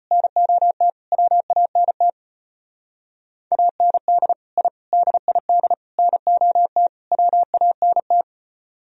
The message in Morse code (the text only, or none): NOT WANT AND I DID NOT WANT